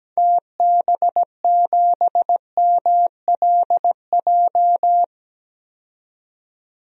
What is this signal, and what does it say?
Morse code, heard as a keyed tone, about 17 wpm, 700 Hz. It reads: TB7MLJ
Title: DXCC-100 Call Signs - 6 Characters - 17wpm